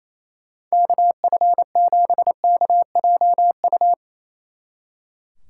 Morse code, heard as a keyed tone, about 34 wpm, 700 Hz. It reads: KF7KJU
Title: Speed Racing - USA Call Signs 28wpm